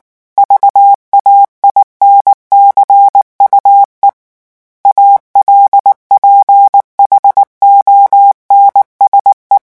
Morse code, heard as a keyed tone, about 19 wpm, 800 Hz. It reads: VAINCUE ALPHONSE
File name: morse mp3 / revision9 citations humoristiques vitesse 18 mots minute